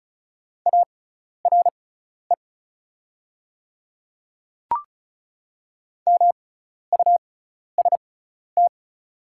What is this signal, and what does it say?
Morse code, heard as a keyed tone, about 36 wpm, spaced as for 12 wpm, 690 Hz. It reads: ARE MUST
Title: 6x Character Spacing - Top 500 Words - 35wpm